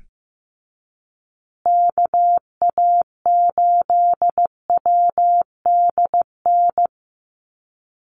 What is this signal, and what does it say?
Morse code, heard as a keyed tone, about 15 wpm, 700 Hz. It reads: KA8WDN
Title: USA Call Signs Repeated 3x 15wpm